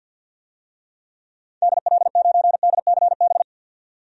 Morse code, heard as a keyed tone, about 52 wpm, 700 Hz. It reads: DL9DCB
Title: DXCC-100 Call Signs - 6 Characters - 50wpm